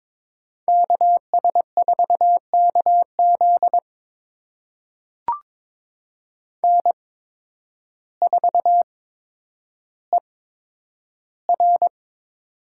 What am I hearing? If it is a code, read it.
Morse code, heard as a keyed tone, about 22 wpm, spaced as for 10 wpm, 700 Hz. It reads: KS4KZ N4ER
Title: USA Call Signs - 8x Character Spacing - 22wpm